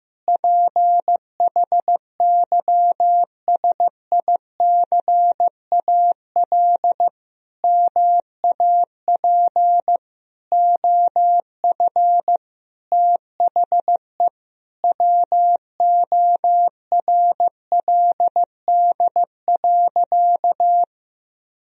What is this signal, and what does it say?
Morse code, heard as a keyed tone, about 16 wpm, 700 Hz. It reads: PHYSICAL MAP OF THE WORLD.
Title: Sentences from Top 1000 Words - Review Part 4 - 15wpm